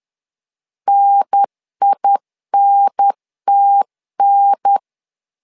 Morse code, heard as a keyed tone, about 10 wpm, 800 Hz. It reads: NINTN
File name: LEZIONI / 2 UHDN / 2 UHDN 10wpm 1904